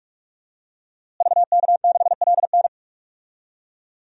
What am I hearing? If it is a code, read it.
Morse code, heard as a keyed tone, about 45 wpm, 700 Hz. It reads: VK6LN
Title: DXCC-100 Call Signs - 5 Characters - Repeated 3x - 45wpm